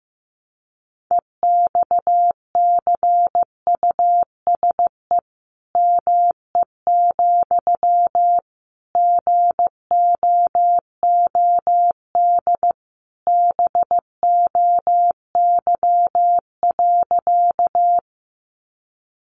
Morse code, heard as a keyed tone, about 14 wpm, 700 Hz. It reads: EXCUSE ME, GOOD BOY.
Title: Sentences from Top 700 Words 15wpm